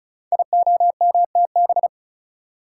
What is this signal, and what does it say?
Morse code, heard as a keyed tone, about 36 wpm, 700 Hz. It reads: IOMTB
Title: Code Groups - 5 Letters - 35wpm